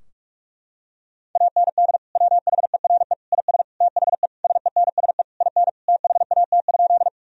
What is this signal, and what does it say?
Morse code, heard as a keyed tone, about 46 wpm, 700 Hz. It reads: AND WHERE IS THE SENSE IN THAT?
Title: Sentences from Top 1000 Words - Review Part 4 - 45wpm